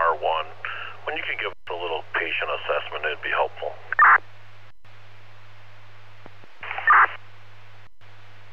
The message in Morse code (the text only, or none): none